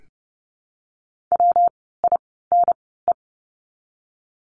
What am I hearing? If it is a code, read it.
Morse code, heard as a keyed tone, about 30 wpm, spaced as for 17 wpm, 700 Hz. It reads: WINE
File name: course / 30:10wpm / 115 Lesson 19f - Letter w - Words 30wpm